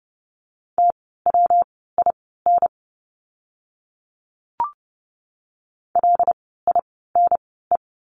Morse code, heard as a keyed tone, about 30 wpm, spaced as for 17 wpm, 700 Hz. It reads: TWIN LINE
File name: course / 30:10wpm / 115 Lesson 19f - Letter w - Words 30wpm